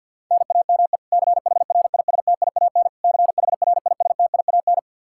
Morse code, heard as a keyed tone, about 50 wpm, 700 Hz. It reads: NAME CHRISTIAN CHRISTIAN
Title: Practice QSO - Exchange 1 - Beginning - 50wpm